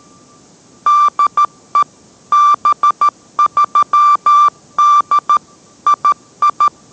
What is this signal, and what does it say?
Morse code, heard as a keyed tone, about 13 wpm, 1180 Hz. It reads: DE B3D II